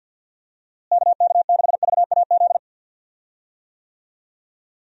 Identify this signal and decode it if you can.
Morse code, heard as a keyed tone, about 51 wpm, 700 Hz. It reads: KK6VAZ
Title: USA Call Signs - 6 Characters - Repeated 3x - 50wpm